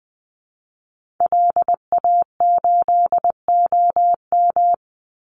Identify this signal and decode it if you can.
Morse code, heard as a keyed tone, about 20 wpm, 700 Hz. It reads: LA8OM
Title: Call Signs - 20wpm